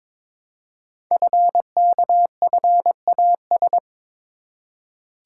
Morse code, heard as a keyed tone, about 22 wpm, 690 Hz. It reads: FKFAS
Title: Code Groups - 5 Letter:Numbers - 22wpm